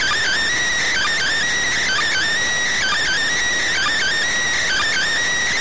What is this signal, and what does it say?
Emergency Alert System (SAME) tones (header bursts), heard as harsh buzzing bursts. It is heard from 0.8 to 5.1 s.